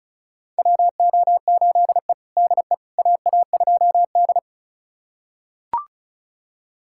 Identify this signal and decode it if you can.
Morse code, heard as a keyed tone, about 36 wpm, 700 Hz. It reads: WO8E DE AA2D
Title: USA Callsign DE Callsign - 4 Characters - 35wpm